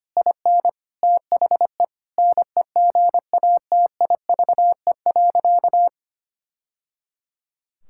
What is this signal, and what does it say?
Morse code, heard as a keyed tone, about 25 wpm, 700 Hz. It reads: IN THE NEGATIVE.
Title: Sentences from Top 1000 Words - Review Part 6 - 25wpm